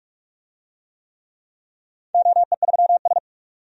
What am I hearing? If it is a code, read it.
Morse code, heard as a keyed tone, about 45 wpm, 700 Hz. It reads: OE3S
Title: DXCC-100 Call Signs - 4 Characters - 45wpm